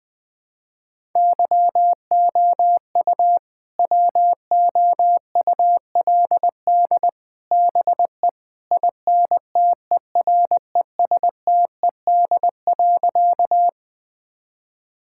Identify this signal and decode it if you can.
Morse code, heard as a keyed tone, about 20 wpm, 700 Hz. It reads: YOU WOULD BE INTERESTED.
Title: Sentences from Top 1000 Words - Review Part 6 - 20wpm